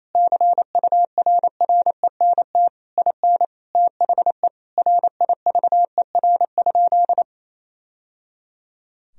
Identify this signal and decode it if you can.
Morse code, heard as a keyed tone, about 28 wpm, 700 Hz. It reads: CURRENT IN THE RIVER?
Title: Sentences from Top 1000 Words - Review Part 4 - 28wpm